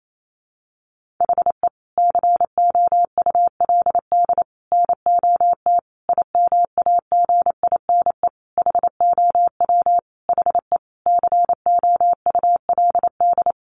Morse code, heard as a keyed tone, about 28 wpm, 700 Hz. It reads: HE COULD NOT IMAGINE HOW HE COULD
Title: Sentences from Top 1400 Words 28wpm